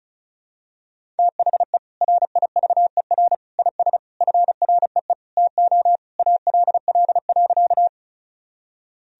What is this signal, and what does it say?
Morse code, heard as a keyed tone, about 34 wpm, 700 Hz. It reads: THE RIVER IS FREE TO ALL.
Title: Sentences from Top 1000 Words - Review Part 1 - 35wpm